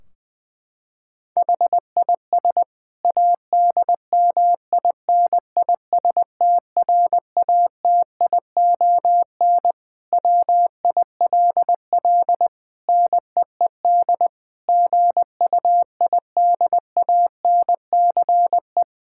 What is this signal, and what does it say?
Morse code, heard as a keyed tone, about 20 wpm, 700 Hz. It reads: HIS ADMINISTRATION WILL NEED GUIDANCE
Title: Sentences from Top 1600 Words 20wpm